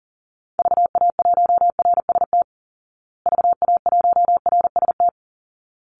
Morse code, heard as a keyed tone, about 40 wpm, 700 Hz. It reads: VA1RST VA1RST
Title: Call Signs Repeated 3x - 40wpm